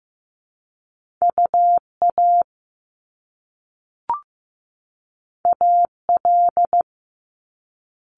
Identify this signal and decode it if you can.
Morse code, heard as a keyed tone, about 15 wpm, 700 Hz. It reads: UA AL
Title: Top Two Letter Combinations in Words 15wpm